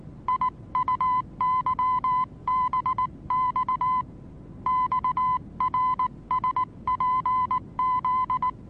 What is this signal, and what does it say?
Morse code, heard as a keyed tone, about 19 wpm, 1010 Hz. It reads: IUYBX XRSPZ
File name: repos / Morsekurs / DJ4UF Morsen26